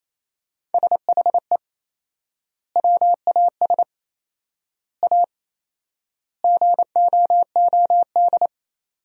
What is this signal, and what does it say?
Morse code, heard as a keyed tone, about 28 wpm, 700 Hz. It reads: SHE WAS A GOOD
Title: 4x Word-Spacing - Sentences from Top 400 Words - 28wpm